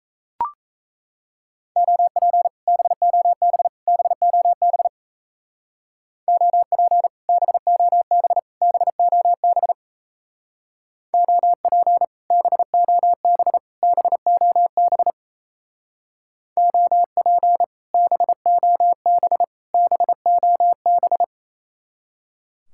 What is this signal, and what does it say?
Morse code, heard as a keyed tone, about 34 wpm, 700 Hz. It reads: OP BOB BOB OP BOB BOB OP BOB BOB OP BOB BOB
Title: Speed-Racing Common QSO Phrases - 28wpm